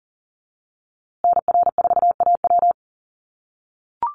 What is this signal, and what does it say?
Morse code, heard as a keyed tone, about 37 wpm, 700 Hz. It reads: NR4AW
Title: USA Call Signs Repeated 3x 40wpm